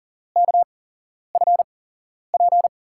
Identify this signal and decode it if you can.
Morse code, heard as a keyed tone, about 40 wpm, spaced as for 11 wpm, 700 Hz. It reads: KFP
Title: USA Call Signs - 5 Characters - 8x Character Spacing - 40wpm